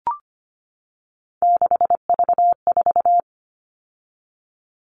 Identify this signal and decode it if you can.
Morse code, heard as a keyed tone, about 25 wpm, 700 Hz. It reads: 6V4
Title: All Cut Numbers - 3-Digit 25wpm